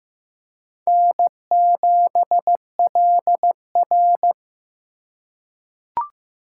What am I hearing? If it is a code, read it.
Morse code, heard as a keyed tone, about 15 wpm, 700 Hz. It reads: N7LR
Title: K1USN SST QSO Practice - 15wpm